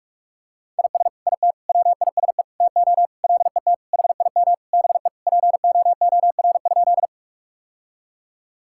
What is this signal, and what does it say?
Morse code, heard as a keyed tone, about 46 wpm, 700 Hz. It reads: IS IT WISE TO LET HIM BE POOR?
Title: Sentences from Top 1600 Words 45wpm